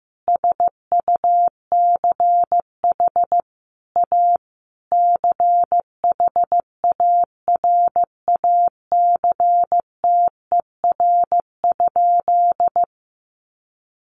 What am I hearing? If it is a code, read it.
Morse code, heard as a keyed tone, about 15 wpm, 700 Hz. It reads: SUCH A CHARACTER?
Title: Sentences from Top 1200 Words 15wpm